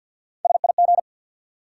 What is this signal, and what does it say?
Morse code, heard as a keyed tone, about 53 wpm, 700 Hz. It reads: SIG
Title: Top 90 QSO Elements 50wpm